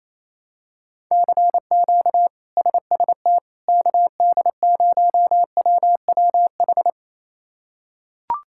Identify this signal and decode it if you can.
Morse code, heard as a keyed tone, about 28 wpm, 700 Hz. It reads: CQ SST KD0WWH
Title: K1USN SST QSO Practice - 28wpm